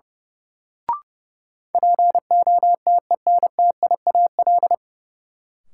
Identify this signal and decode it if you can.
Morse code, heard as a keyed tone, about 30 wpm, 700 Hz. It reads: POTENTIAL
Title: Sentences from Top 1000 Words - Review Part 4 - New Word Drill - 30wpm